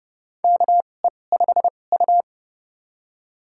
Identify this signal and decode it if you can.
Morse code, heard as a keyed tone, about 30 wpm, spaced as for 21 wpm, 700 Hz. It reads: KE5U
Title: USA Call Signs - 4 Characters - 2x Character Spacing - 30wpm